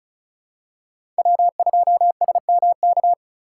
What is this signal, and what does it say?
Morse code, heard as a keyed tone, about 34 wpm, 700 Hz. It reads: W2SMK
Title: USA Call Signs - 5 Characters - 4x Character Spacing - 35wpm